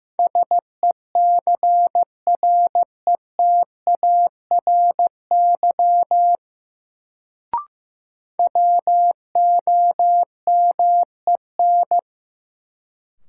Morse code, heard as a keyed tone, about 15 wpm, 700 Hz. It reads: SECRETARY WOMEN
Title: Sentences from Top 1300 Words - New Word Drill Alt - 15wpm